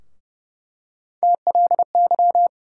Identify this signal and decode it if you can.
Morse code, heard as a keyed tone, about 30 wpm, 700 Hz. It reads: TLY
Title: USA Call Sign Postfixes - 3 Letters 30wpm